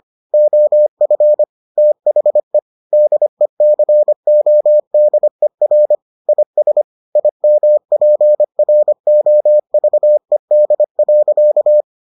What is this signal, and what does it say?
Morse code, heard as a keyed tone, about 25 wpm, 600 Hz. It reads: OF THE DECODER IS IMPROVED.